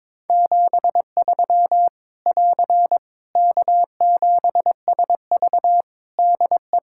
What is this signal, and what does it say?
Morse code, heard as a keyed tone, about 22 wpm, 700 Hz. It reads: 73 <AR> K7SV DE
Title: Practice QSO - Exchange 4 - Ending - 22wpm